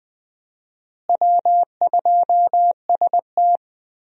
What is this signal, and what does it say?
Morse code, heard as a keyed tone, about 20 wpm, 700 Hz. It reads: W2ST
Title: USA Call Signs - 4 Characters - Repeated 2x - 20wpm